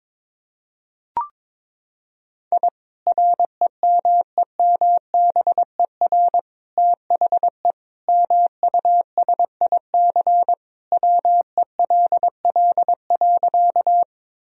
Morse code, heard as a keyed tone, about 21 wpm, 700 Hz. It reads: I REMEMBER THE MUSIC WELL.
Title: Sentences from Top 1000 Words - Review Part 1 - 22wpm